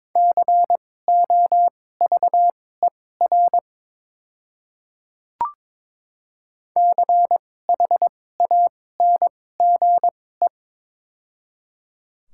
Morse code, heard as a keyed tone, about 23 wpm, spaced as for 16 wpm, 700 Hz. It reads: COVER CHANGE
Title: 2x Character Spacing - Top 300 Words - 22wpm